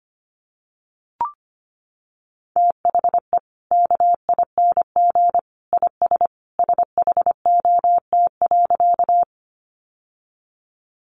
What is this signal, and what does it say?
Morse code, heard as a keyed tone, about 25 wpm, 700 Hz. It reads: THE KING IS SHOT.
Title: Sentences from Top 1100 Words 25wpm